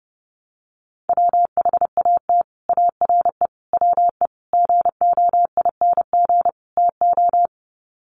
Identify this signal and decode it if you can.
Morse code, heard as a keyed tone, about 30 wpm, 700 Hz. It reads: WHAT ARE WE GOING TO
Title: Sets of 5 Words 30wpm